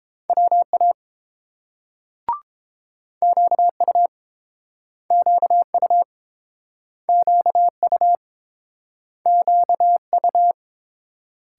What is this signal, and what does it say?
Morse code, heard as a keyed tone, about 30 wpm, 700 Hz. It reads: WA QU QU QU QU